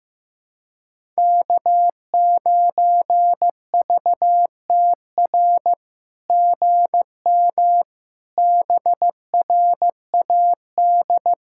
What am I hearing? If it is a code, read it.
Morse code, heard as a keyed tone, about 16 wpm, 700 Hz. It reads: K9VTR GM BRAD
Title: SOTA QSO Practice 15wpm